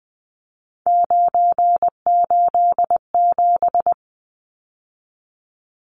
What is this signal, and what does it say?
Morse code, heard as a keyed tone, about 20 wpm, 700 Hz. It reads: 987
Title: Numbers - 3-Digit 20wpm